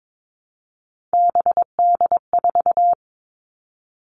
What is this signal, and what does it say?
Morse code, heard as a keyed tone, about 22 wpm, 700 Hz. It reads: BD4
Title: Callsign Prefixes for Top 100 DXCC - 22wpm